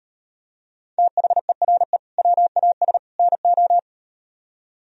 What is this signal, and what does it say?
Morse code, heard as a keyed tone, about 38 wpm, 700 Hz. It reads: THERE WAS NO